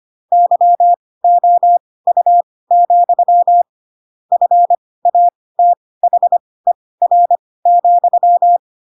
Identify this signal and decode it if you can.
Morse code, heard as a keyed tone, about 25 wpm, spaced as for 18 wpm, 700 Hz. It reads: YOU, FATHER,